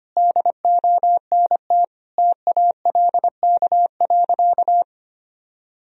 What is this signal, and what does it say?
Morse code, heard as a keyed tone, about 25 wpm, 700 Hz. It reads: DONT TALK.
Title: Sentences from Top 1000 Words - Review Part 2 - 25wpm